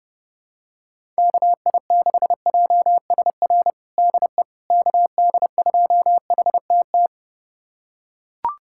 Morse code, heard as a keyed tone, about 30 wpm, 700 Hz. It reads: KI6JSR DE KD2HTT
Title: USA Callsign DE Callsign - 6 Characters - 30wpm